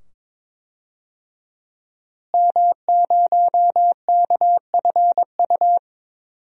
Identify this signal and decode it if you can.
Morse code, heard as a keyed tone, about 22 wpm, 700 Hz. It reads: M0KFU
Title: DXCC-100 Call Signs - 5 Characters - Repeated 2x - 22wpm